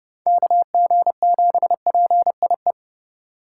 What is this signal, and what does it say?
Morse code, heard as a keyed tone, about 29 wpm, 700 Hz. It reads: KG7PIE